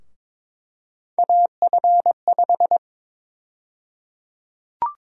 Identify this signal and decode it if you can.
Morse code, heard as a keyed tone, about 22 wpm, 700 Hz. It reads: AF5
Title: USA Call Sign Prefixes - Repeated 2x - 22wpm